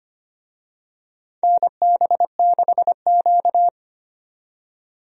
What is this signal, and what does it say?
Morse code, heard as a keyed tone, about 25 wpm, 700 Hz. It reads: NB6Q
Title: USA Call Signs - 4 Characters - 25wpm